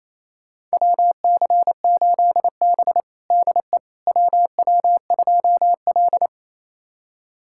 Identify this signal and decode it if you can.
Morse code, heard as a keyed tone, about 28 wpm, 700 Hz. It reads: WC8B DE WW2L
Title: USA Callsign DE Callsign - 4 Characters - 28wpm